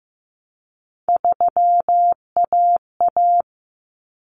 Morse code, heard as a keyed tone, about 15 wpm, 700 Hz. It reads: 3AA
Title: Common Cut Numbers - 3-Digit 15wpm